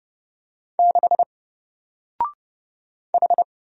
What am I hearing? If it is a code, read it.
Morse code, heard as a keyed tone, about 29 wpm, 700 Hz. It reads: 6 H
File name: Single-Letter-Number-30wpm